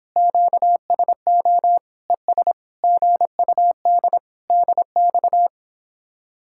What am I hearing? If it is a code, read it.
Morse code, heard as a keyed tone, about 29 wpm, 700 Hz. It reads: QSO ES GUD DX